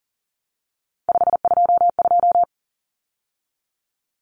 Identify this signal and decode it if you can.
Morse code, heard as a keyed tone, about 40 wpm, 700 Hz. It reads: 522
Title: Numbers - 3-Digit 40wpm